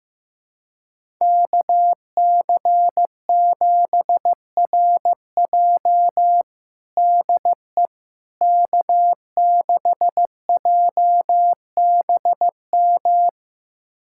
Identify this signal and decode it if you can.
Morse code, heard as a keyed tone, about 15 wpm, 700 Hz. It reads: KC7RJ DE K6JBM
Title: USA Callsign DE Callsign - 5 Characters - 15wpm